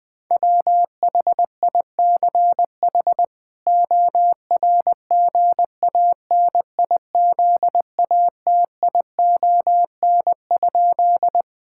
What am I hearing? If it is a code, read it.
Morse code, heard as a keyed tone, about 21 wpm, 700 Hz. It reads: WHICH ORGANIZATION?
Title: Sentences from Top 1000 Words - Review Part 5 - 20wpm